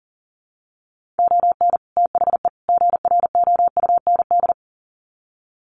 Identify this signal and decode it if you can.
Morse code, heard as a keyed tone, about 40 wpm, 700 Hz. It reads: ON THE GROUND
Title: Sets of 3 Words 40wpm